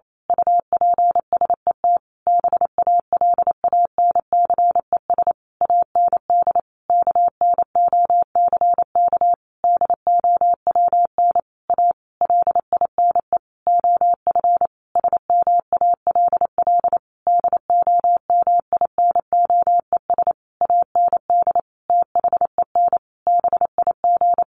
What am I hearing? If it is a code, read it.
Morse code, heard as a keyed tone, about 28 wpm, 700 Hz. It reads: UPSET BALANCES AND KNOCK DOWN A LINE OF SMALL DOMINOES AND THEN BIG
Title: A Sound Of Thunder 28wpm